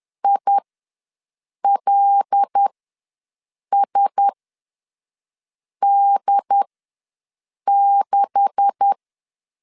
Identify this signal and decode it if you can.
Morse code, heard as a keyed tone, about 10 wpm, spaced as for 5 wpm, 800 Hz. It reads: ILSD6
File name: LEZIONI / 9 286 ws 10wpm 4032